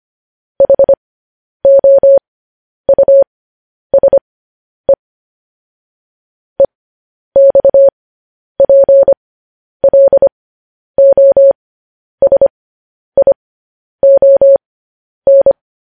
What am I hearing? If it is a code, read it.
Morse code, heard as a keyed tone, about 25 wpm, spaced as for 10 wpm, 550 Hz. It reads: HOUSE EXPLOSION
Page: News Headlines in Morse Code at 10 WPM